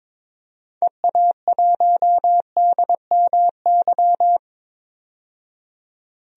Morse code, heard as a keyed tone, about 22 wpm, 700 Hz. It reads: EA1DMY
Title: DXCC-100 Call Signs - 6 Characters - 22wpm